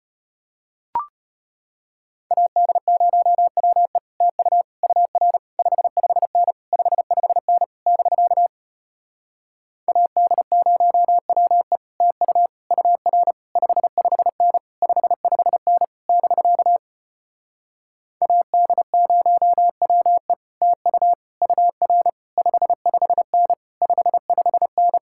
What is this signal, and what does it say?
Morse code, heard as a keyed tone, about 33 wpm, 700 Hz. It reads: AD0WE TU UR 55N 55N <BK> AD0WE TU UR 55N 55N <BK> AD0WE TU UR 55N 55N